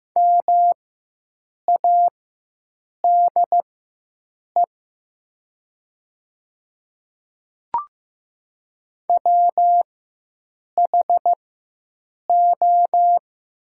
Morse code, heard as a keyed tone, about 15 wpm, spaced as for 7 wpm, 700 Hz. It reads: MADE WHO
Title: 4x Character Spacing - Top 200 Words - 15wpm